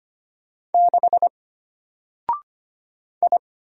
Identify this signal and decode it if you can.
Morse code, heard as a keyed tone, about 25 wpm, 700 Hz. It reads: 6 I
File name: Single-Letter-Number-25wpm